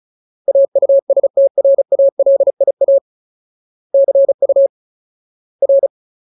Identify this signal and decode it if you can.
Morse code, heard as a keyed tone, about 34 wpm, 550 Hz. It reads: AUSTRALIA CU R
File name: Ultimate Word List / 35WPM Ultimate